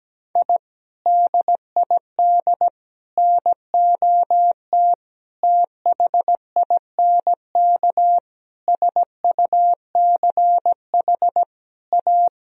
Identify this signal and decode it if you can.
Morse code, heard as a keyed tone, about 17 wpm, 700 Hz. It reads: I DID NOT THINK SUCH A
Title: Sentences from Top 1000 Words - Review Part 1 - 17wpm